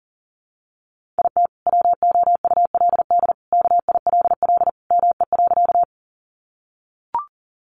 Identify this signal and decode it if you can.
Morse code, heard as a keyed tone, about 40 wpm, 700 Hz. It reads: IT WOULD KILL ME.
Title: Sentences from Top 1400 Words 40wpm